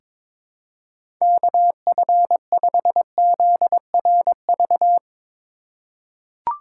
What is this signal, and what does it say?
Morse code, heard as a keyed tone, about 22 wpm, 700 Hz. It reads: KF5ZRV
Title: USA Call Signs - 6 Characters - 4x Character Spacing - 22wpm